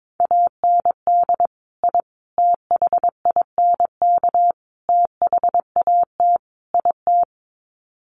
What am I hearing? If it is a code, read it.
Morse code, heard as a keyed tone, about 22 wpm, 700 Hz. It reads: AND I THINK THAT IT
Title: Sets of 5 Words Encore2 22wpm